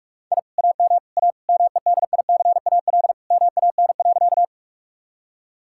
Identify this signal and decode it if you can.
Morse code, heard as a keyed tone, about 41 wpm, 700 Hz. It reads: I AM A MEDICAL MAN.
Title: Sentences from Top 1000 Words - Review Part 5 - 45wpm